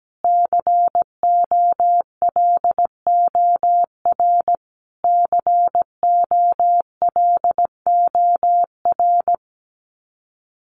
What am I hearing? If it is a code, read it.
Morse code, heard as a keyed tone, about 17 wpm, 700 Hz. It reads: COLOR COLOR
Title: Top 300 Words Repeated 17wpm